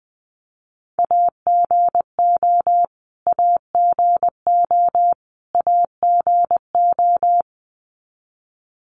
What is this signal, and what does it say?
Morse code, heard as a keyed tone, about 20 wpm, 700 Hz. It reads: AGO AGO AGO
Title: Top 500 Words Repeated 20wpm